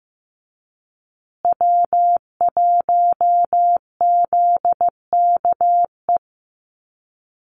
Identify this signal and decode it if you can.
Morse code, heard as a keyed tone, about 15 wpm, 700 Hz. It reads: W1ZKE
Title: USA Call Signs Repeated 3x 15wpm